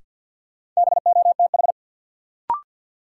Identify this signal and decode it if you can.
Morse code, heard as a keyed tone, about 48 wpm, 700 Hz. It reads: BOTH
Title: Sets of 4 Words - New Word Drill - 50wpm